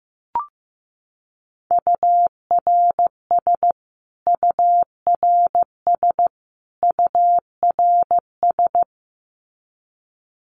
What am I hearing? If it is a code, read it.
Morse code, heard as a keyed tone, about 15 wpm, 700 Hz. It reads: URS URS URS
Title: Abbreviations Repeated 3x - 15wpm